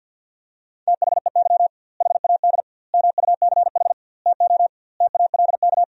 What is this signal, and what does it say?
Morse code, heard as a keyed tone, about 50 wpm, 700 Hz. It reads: THEY HAD MUCH TO TALK
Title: 2x Word-Spacing - Sentences from Top 400 Words - 50wpm